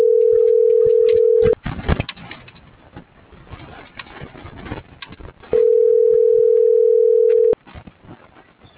Ringback tone (North America), 440 + 480 Hz.